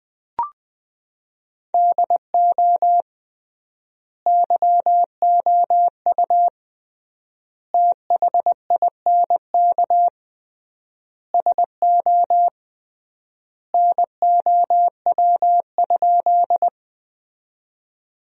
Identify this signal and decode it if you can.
Morse code, heard as a keyed tone, about 20 wpm, 700 Hz. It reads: DO YOU THINK SO NOW?
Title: 3x Word-Spacing - Sentences from Top 200 Words - 20wpm